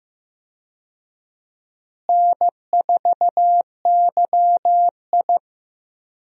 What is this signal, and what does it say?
Morse code, heard as a keyed tone, about 15 wpm, 700 Hz. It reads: N4YI
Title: USA Call Signs - 4 Characters - Repeated 2x - 15wpm